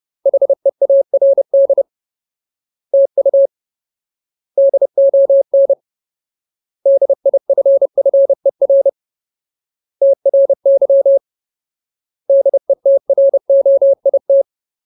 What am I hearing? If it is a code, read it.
Morse code, heard as a keyed tone, about 30 wpm, 550 Hz. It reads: HEARD TU DON DIFFER TRY DETROIT
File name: Ultimate Word List / 30WPM Ultimate